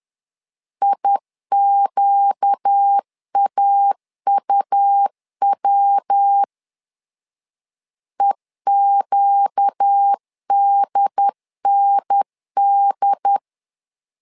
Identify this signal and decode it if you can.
Morse code, heard as a keyed tone, about 10 wpm, 800 Hz. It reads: IQAUW EQDND
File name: LEZIONI / 4 Q 10wpm 2210